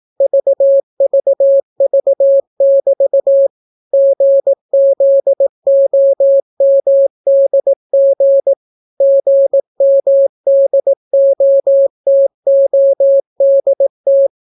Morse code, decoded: VVV<BT> GZOMDG GMDOTODT

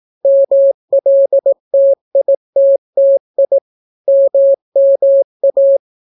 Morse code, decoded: MLTITTI MMA

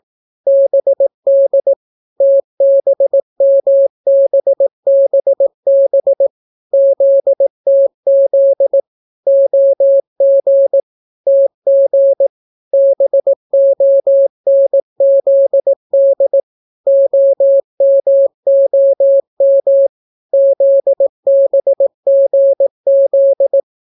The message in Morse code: BD TBMBBB ZTZ OG TG BONZD OMOM ZBGZ

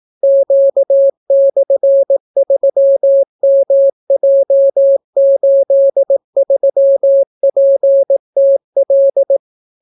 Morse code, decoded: Q/3MJ83PTL

